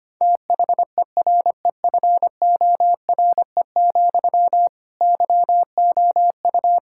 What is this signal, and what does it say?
Morse code, heard as a keyed tone, about 25 wpm, 700 Hz. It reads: THEREFORE, YOU
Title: Sentences from Top 1000 Words - Review Part 5 - 25wpm